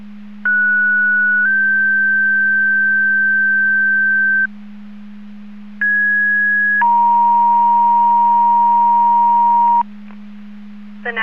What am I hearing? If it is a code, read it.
Two-tone page, heard as a one-second tone, then a three-second tone, about 1470 then 1550 Hz, about 1650 then 940 Hz. It is heard from 0.4 to 9.8 s.
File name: Portage 154.385 Paging